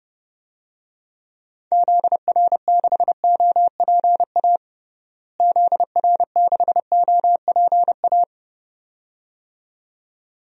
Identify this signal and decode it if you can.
Morse code, heard as a keyed tone, about 30 wpm, 700 Hz. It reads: ZR6OPA ZR6OPA
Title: DXCC-100 Call Signs - 6 Characters - Repeated 2x - 30wpm